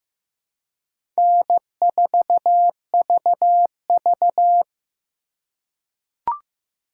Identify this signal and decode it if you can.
Morse code, heard as a keyed tone, about 15 wpm, 700 Hz. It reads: N4VV
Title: USA Call Signs - 4x Character Spacing - 15wpm